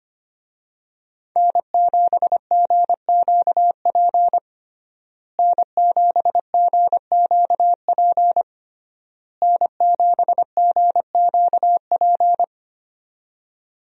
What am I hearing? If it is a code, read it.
Morse code, heard as a keyed tone, about 25 wpm, 700 Hz. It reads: N7GQP N7GQP N7GQP
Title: USA Call Signs - 5 Characters - Repeated 3x - 25wpm